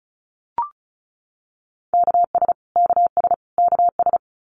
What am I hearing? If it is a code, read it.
Morse code, heard as a keyed tone, about 34 wpm, 700 Hz. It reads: KS KS KS
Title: US States Repeated 3x 35wpm